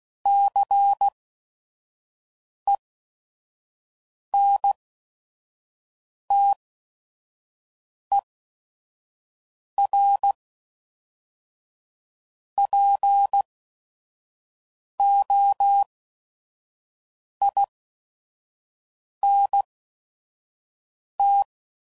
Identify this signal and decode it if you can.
Morse code, heard as a keyed tone, about 16 wpm, spaced as for 5 wpm, 800 Hz. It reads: CENTERPOINT
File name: CW Code Groups @5WPM / fromSM's / Cities